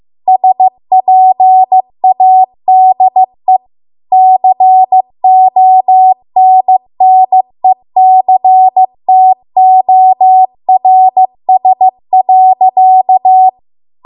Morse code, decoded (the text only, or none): SPADE CONNECTORS.